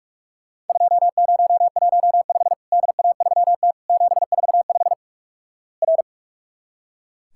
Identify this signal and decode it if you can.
Morse code, heard as a keyed tone, about 44 wpm, 700 Hz. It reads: 2015 DA3T 745 R
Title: Worked All Europe DX Contest - QTC Exchange - 45wpm